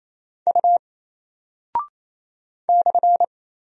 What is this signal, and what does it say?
Morse code, heard as a keyed tone, about 28 wpm, 700 Hz. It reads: U /
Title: Single Character (Letter:Number:Punctuation:Prosign) Alt 28wpm